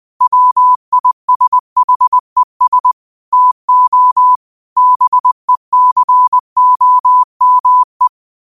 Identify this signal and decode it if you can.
Morse code, heard as a keyed tone, about 20 wpm, 1000 Hz. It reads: WISHES TO BECOME
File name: qotdfiles / 20wpm